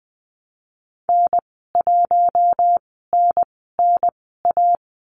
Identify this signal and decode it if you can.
Morse code, heard as a keyed tone, about 20 wpm, spaced as for 14 wpm, 700 Hz. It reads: N1NNA
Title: 062 Lesson 11g - Letter h - Call Signs 20wpm